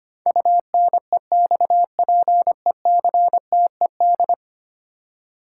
Morse code, heard as a keyed tone, about 25 wpm, 700 Hz. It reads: UNEXPECTED
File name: Ten Letter Words - 25wpm